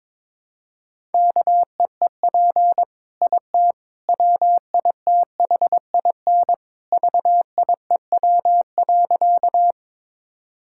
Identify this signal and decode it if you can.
Morse code, heard as a keyed tone, about 22 wpm, 700 Hz. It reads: KEEP IT WITHIN VIEW.
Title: Sentences from Top 1000 Words - Review Part 6 - 22wpm